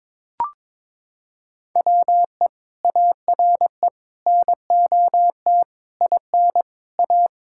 Morse code, heard as a keyed tone, about 22 wpm, 700 Hz. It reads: WE ARE NOT IN A
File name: practice / 22wpm / Sets of 7 Words - 22wpm